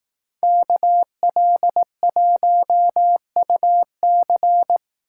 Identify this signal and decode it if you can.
Morse code, heard as a keyed tone, about 18 wpm, 700 Hz. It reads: KL1UC
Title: Speed Racing - USA 2x2 Call Signs 15wpm